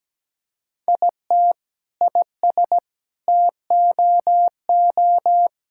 Morse code, heard as a keyed tone, about 17 wpm, 700 Hz. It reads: IT IS TOO
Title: Sentences from Top 1000 Words - Review Part 6 - 17wpm